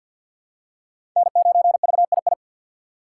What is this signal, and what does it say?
Morse code, heard as a keyed tone, about 51 wpm, 690 Hz. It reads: N9VII
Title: USA Call Signs - 2x Character Spacing - 50wpm